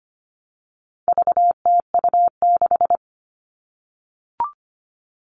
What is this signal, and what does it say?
Morse code, heard as a keyed tone, about 25 wpm, 700 Hz. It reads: VTU6
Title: All Cut Numbers - 4-Digit 25wpm